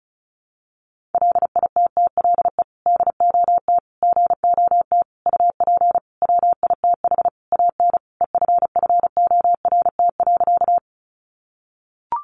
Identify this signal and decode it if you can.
Morse code, heard as a keyed tone, about 34 wpm, 700 Hz. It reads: LITTLE DOT GOT UP WITH AN EFFORT.
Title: Sentences from Top 1500 Words 35wpm